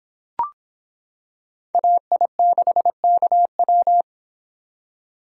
Morse code, heard as a keyed tone, about 26 wpm, 700 Hz. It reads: AI6KW